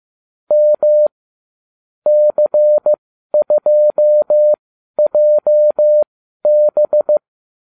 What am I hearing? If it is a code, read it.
Morse code, heard as a keyed tone, about 15 wpm, spaced as for 12 wpm, 600 Hz. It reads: M C2JB